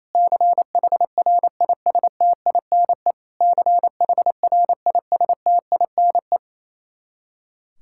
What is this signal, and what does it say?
Morse code, heard as a keyed tone, about 28 wpm, 700 Hz. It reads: CHRISTINE CHRISTINE
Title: Practice QSO - Exchange 1 - Beginning - 28wpm